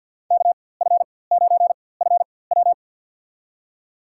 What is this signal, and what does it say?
Morse code, heard as a keyed tone, about 47 wpm, spaced as for 23 wpm, 700 Hz. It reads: KF9FW
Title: USA Call Signs - 4x Character Spacing - 50wpm